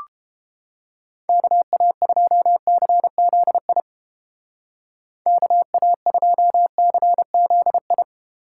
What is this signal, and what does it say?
Morse code, heard as a keyed tone, about 30 wpm, 700 Hz. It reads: KA2CZI KA2CZI